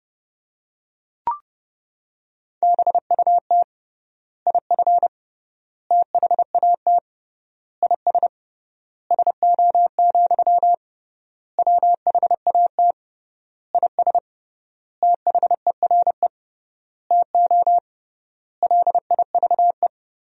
Morse code, heard as a keyed tone, about 30 wpm, 700 Hz. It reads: BUT IF THAT IS SO, WHAT IS THERE TO LIVE
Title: 3x Word-Spacing - Sentences from Top 200 Words - 30wpm